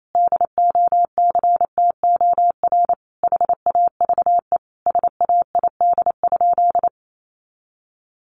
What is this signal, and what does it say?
Morse code, heard as a keyed tone, about 28 wpm, 700 Hz. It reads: DOCTOR HAVE SAID?
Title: Sentences from Top 1500 Words 28wpm